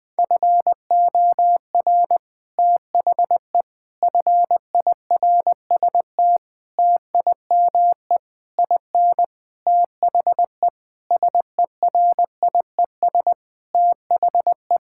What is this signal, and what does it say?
Morse code, heard as a keyed tone, about 20 wpm, 700 Hz. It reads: FOR THE FIRST TIME IN THE SERIES THE
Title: Sentences with 6-Word Phrases - 20wpm